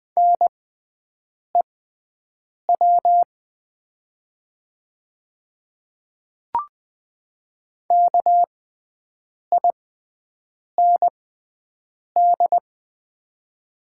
Morse code, heard as a keyed tone, about 20 wpm, spaced as for 7 wpm, 700 Hz. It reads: NEW KIND
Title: 6x Character Spacing - Top 200 Words - 20wpm